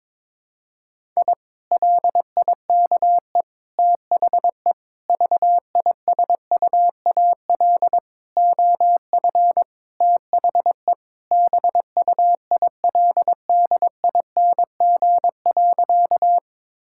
Morse code, heard as a keyed tone, about 21 wpm, 700 Hz. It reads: I LIKE THE VISUAL OF THE BUILDING.
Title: Sentences from Top 1000 Words - Review Part 6 - 22wpm